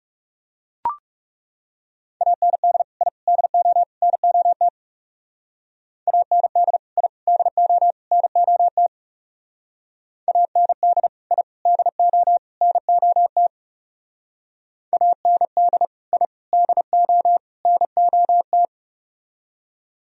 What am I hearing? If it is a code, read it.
Morse code, heard as a keyed tone, about 35 wpm, 700 Hz. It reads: AND I DO NOT AND I DO NOT AND I DO NOT AND I DO NOT
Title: Speed-Racing - Sets of 4 Words 30wpm